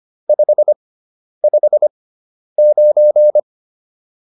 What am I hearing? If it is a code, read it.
Morse code, heard as a keyed tone, about 25 wpm, spaced as for 10 wpm, 600 Hz. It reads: 559